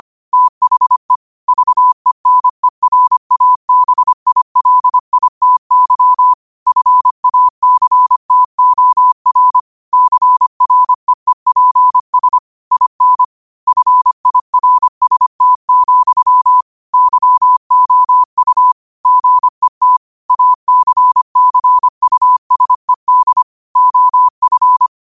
Morse code, heard as a keyed tone, about 25 wpm, 1000 Hz. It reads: THE VENERABILITY FACTOR CREEPS IN FIRST, YOU GET ACCUSED OF
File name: 25wpm